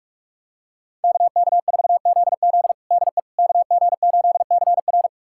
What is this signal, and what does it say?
Morse code, heard as a keyed tone, about 46 wpm, 700 Hz. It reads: KK4ZZ DE KG8CR
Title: USA Callsign DE Callsign - 5 Characters - 45wpm